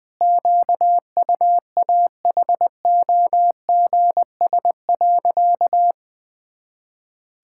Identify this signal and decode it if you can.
Morse code, heard as a keyed tone, about 20 wpm, 700 Hz. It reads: QUAHOGS.
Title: Pangram Challenge v2 - 20wpm